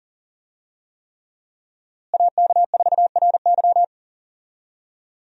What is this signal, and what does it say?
Morse code, heard as a keyed tone, about 39 wpm, 700 Hz. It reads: AK4RY
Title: USA Call Signs - 5 Characters - Repeated 3x - 40wpm